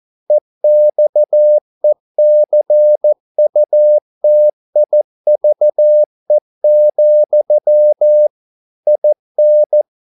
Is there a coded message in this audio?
Yes: Morse code from 0.3 to 9.8 s, heard as a keyed tone, about 14 wpm, 600 Hz. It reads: EXECUTIVE, IN